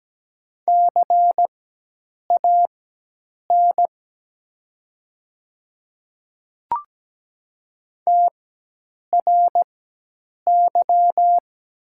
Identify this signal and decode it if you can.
Morse code, heard as a keyed tone, about 17 wpm, spaced as for 8 wpm, 700 Hz. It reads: CAN TRY
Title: 4x Character Spacing - Top 200 Words - 17wpm